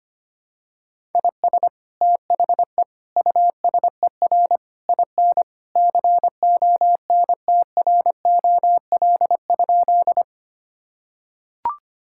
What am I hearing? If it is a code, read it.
Morse code, heard as a keyed tone, about 25 wpm, 700 Hz. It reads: IS THE USER IN CONTROL?
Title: Sentences from Top 1000 Words - Review Part 5 - 25wpm